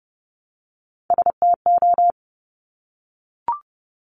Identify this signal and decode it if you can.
Morse code, heard as a keyed tone, about 30 wpm, 700 Hz. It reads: STO
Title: Top Three Letter Combinations in Words 30wpm